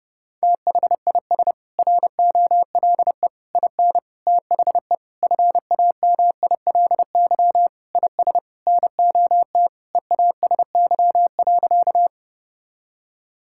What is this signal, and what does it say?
Morse code, heard as a keyed tone, about 32 wpm, 700 Hz. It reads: THIS ROLE IN THE FAMILY IS NOT EASY.
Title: Sentences from Top 1000 Words - Review Part 6 - 30wpm